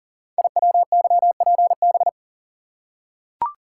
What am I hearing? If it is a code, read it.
Morse code, heard as a keyed tone, about 41 wpm, 700 Hz. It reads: IWYPB
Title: Code Groups - 5 Letters - 40wpm